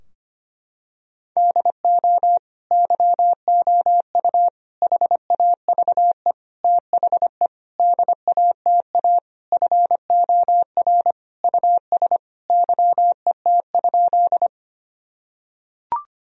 Morse code, heard as a keyed tone, about 25 wpm, 700 Hz. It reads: DO YOU HAVE THE DATA FOR US YET?
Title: Sentences from Top 1000 Words - Review Part 3 - 25wpm